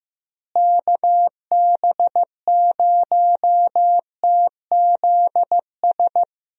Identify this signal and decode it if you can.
Morse code, heard as a keyed tone, about 15 wpm, 700 Hz. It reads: KB0TZS
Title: USA Call Signs - 6 Characters - Repeated 3x - 15wpm